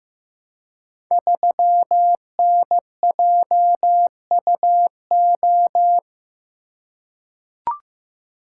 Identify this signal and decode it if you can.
Morse code, heard as a keyed tone, about 15 wpm, 700 Hz. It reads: 3NJUO
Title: Code Groups - 5 Letter:Numbers - 15wpm